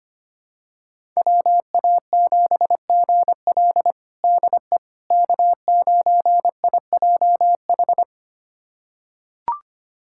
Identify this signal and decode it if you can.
Morse code, heard as a keyed tone, about 25 wpm, 700 Hz. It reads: WA7GL DE K9IJH
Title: USA Callsign DE Callsign - 5 Characters - 25wpm